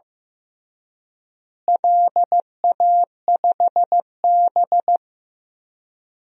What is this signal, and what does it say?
Morse code, heard as a keyed tone, about 15 wpm, 700 Hz. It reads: LA5B
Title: DXCC-100 Call Signs - 4 Characters - Repeated 2x - 15wpm